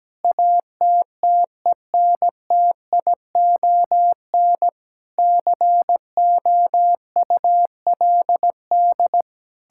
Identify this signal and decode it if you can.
Morse code, heard as a keyed tone, about 17 wpm, 700 Hz. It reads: ATTENTION COULD
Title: Sentences from Top 1000 Words - Review Part 6 - 17wpm